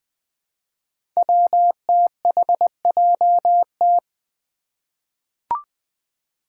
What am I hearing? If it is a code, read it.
Morse code, heard as a keyed tone, about 20 wpm, 700 Hz. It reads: WTHJT
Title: Five Challenging Letter-Number Combinations 20wpm